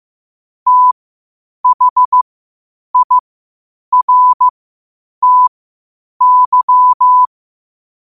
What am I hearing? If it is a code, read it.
Morse code, heard as a keyed tone, about 15 wpm, spaced as for 8 wpm, 1000 Hz. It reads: THIRTY
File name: qotdfiles / 8wpm